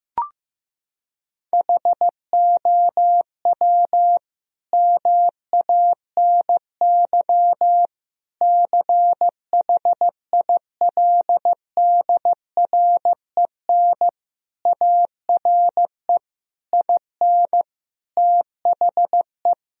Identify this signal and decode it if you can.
Morse code, heard as a keyed tone, about 15 wpm, 700 Hz. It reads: HOW MANY CHILDREN ARE IN THE
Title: Sentences from Top 1000 Words - Review Part 4 - 15wpm